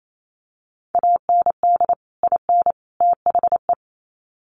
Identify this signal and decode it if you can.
Morse code, heard as a keyed tone, about 28 wpm, 700 Hz. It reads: AND IN THE
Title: Sets of 3 Words 28wpm